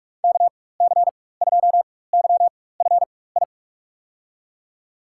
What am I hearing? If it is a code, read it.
Morse code, heard as a keyed tone, about 47 wpm, spaced as for 21 wpm, 700 Hz. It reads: KC2YFI
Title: USA Call Signs - 4x Character Spacing - 45wpm